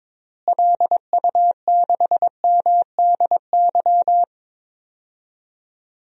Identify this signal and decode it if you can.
Morse code, heard as a keyed tone, about 22 wpm, 700 Hz. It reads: LU6MDY
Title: DXCC-100 Call Signs - 6 Characters - Repeated 3x - 22wpm